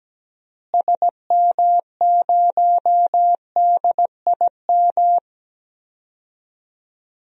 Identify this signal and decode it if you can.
Morse code, heard as a keyed tone, about 17 wpm, 700 Hz. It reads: SM0DIM
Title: DXCC-100 Call Signs - 6 Characters - Repeated 2x - 17wpm